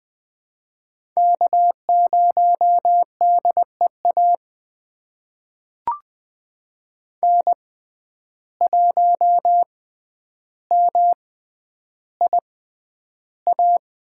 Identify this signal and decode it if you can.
Morse code, heard as a keyed tone, about 20 wpm, spaced as for 10 wpm, 700 Hz. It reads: K0DEA N1MIA
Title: USA Call Signs - 6x Character Spacing - 20wpm